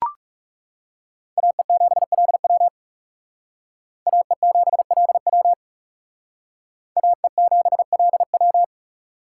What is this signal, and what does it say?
Morse code, heard as a keyed tone, about 37 wpm, 700 Hz. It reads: AE7LW AE7LW AE7LW
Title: Speed Racing - USA 2x2 Call Signs 30wpm